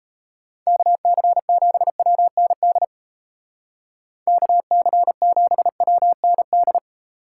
Morse code, heard as a keyed tone, about 36 wpm, 700 Hz. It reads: KC7WND KC7WND